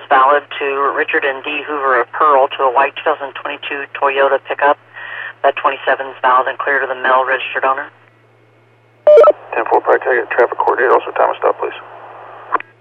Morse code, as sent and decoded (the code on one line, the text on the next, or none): none